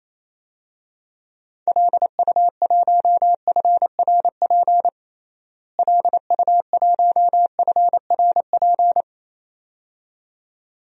Morse code, heard as a keyed tone, about 28 wpm, 700 Hz. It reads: LU1FRP LU1FRP
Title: DXCC-100 Call Signs - 6 Characters - Repeated 2x - 28wpm